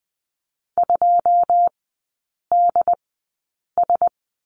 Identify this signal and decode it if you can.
Morse code, heard as a keyed tone, about 20 wpm, spaced as for 8 wpm, 690 Hz. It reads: 2DS
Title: 081 Lesson 14e - Number 2 - Three Characters 20wpm